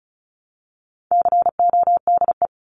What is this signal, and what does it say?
Morse code, heard as a keyed tone, about 34 wpm, 700 Hz. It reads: CODE
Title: Top 1501-1750 Words 35wpm